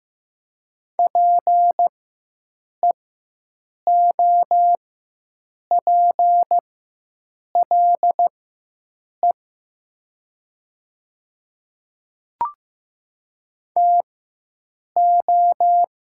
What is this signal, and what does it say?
Morse code, heard as a keyed tone, about 15 wpm, spaced as for 7 wpm, 700 Hz. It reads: PEOPLE TO
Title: 4x Character Spacing - Top 500 Words - 15wpm